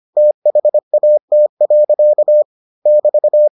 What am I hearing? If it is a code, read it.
Morse code, heard as a keyed tone, about 25 wpm, 600 Hz. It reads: THAT. <BT>